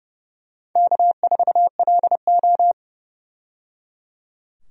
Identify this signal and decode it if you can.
Morse code, heard as a keyed tone, about 30 wpm, 700 Hz. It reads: K4LO
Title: USA Call Signs - 4 Characters - 30wpm